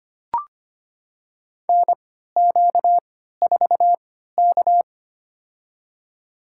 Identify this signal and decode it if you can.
Morse code, heard as a keyed tone, about 25 wpm, spaced as for 14 wpm, 700 Hz. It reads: NQ4K
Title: USA Call Signs - 4 Characters - 3x Character Spacing - 25wpm